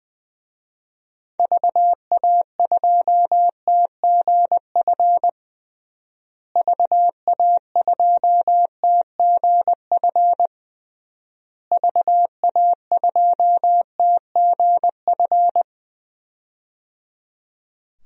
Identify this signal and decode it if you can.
Morse code, heard as a keyed tone, about 20 wpm, 700 Hz. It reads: VA2TGF VA2TGF VA2TGF
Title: DXCC-100 Call Signs - 6 Characters - Repeated 3x - 20wpm